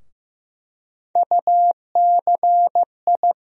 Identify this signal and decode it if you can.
Morse code, heard as a keyed tone, about 15 wpm, 700 Hz. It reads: UCI
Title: USA Call Sign Postfixes - 3 Letters 15wpm